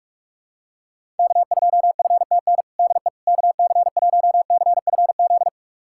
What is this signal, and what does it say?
Morse code, heard as a keyed tone, about 46 wpm, 700 Hz. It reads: K2FTN DE KC1CFZ
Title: USA Callsign DE Callsign 45wpm